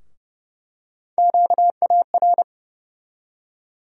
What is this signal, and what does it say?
Morse code, heard as a keyed tone, about 30 wpm, 700 Hz. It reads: QAR
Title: USA Call Sign Postfixes - 3 Letters 30wpm